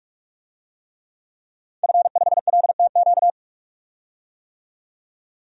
Morse code, heard as a keyed tone, about 42 wpm, 700 Hz. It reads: F5LTQ
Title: DXCC-100 Call Signs - 5 Characters - 45wpm